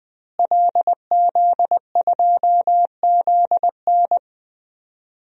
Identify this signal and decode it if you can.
Morse code, heard as a keyed tone, about 20 wpm, 700 Hz. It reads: LZ2ZN
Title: DXCC-100 Call Signs - 5 Characters - Repeated 3x - 20wpm